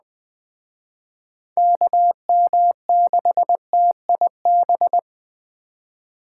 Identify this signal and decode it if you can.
Morse code, heard as a keyed tone, about 20 wpm, 700 Hz. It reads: KM6TIB
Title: Speed Racing - USA 2x3 Call Signs 17wpm